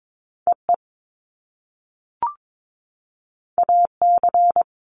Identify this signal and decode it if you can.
Morse code, heard as a keyed tone, about 22 wpm, 700 Hz. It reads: EE AC
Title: Top Two Letter Combinations in Words 22wpm